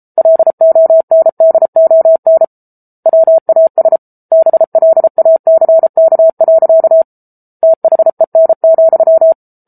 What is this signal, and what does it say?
Morse code, heard as a keyed tone, about 34 wpm, 650 Hz. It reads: LONDON WAS BLACK. THEN,
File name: Book Two / 07-Chapter 7 - 33 WPM